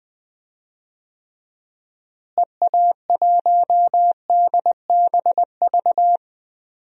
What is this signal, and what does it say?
Morse code, heard as a keyed tone, about 20 wpm, 700 Hz. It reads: EA1DBV
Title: DXCC-100 Call Signs - 6 Characters - Repeated 3x - 20wpm